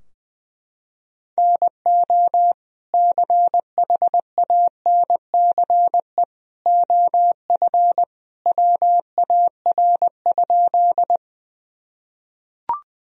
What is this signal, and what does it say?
Morse code, heard as a keyed tone, about 20 wpm, 700 Hz. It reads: NO CHANCE OF WAR?
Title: Sentences from Top 1000 Words - Review Part 2 - 20wpm